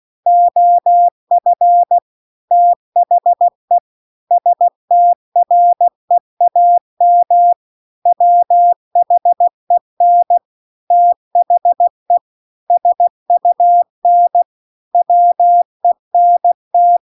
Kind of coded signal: Morse code